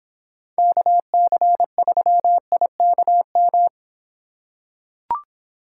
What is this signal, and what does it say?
Morse code, heard as a keyed tone, about 25 wpm, 700 Hz. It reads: KC3IKM